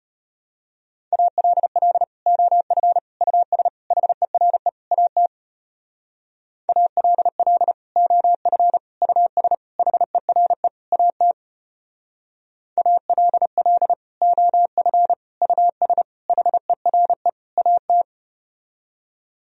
Morse code, heard as a keyed tone, about 33 wpm, 700 Hz. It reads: ALL OF US HERE AT ALL OF US HERE AT ALL OF US HERE AT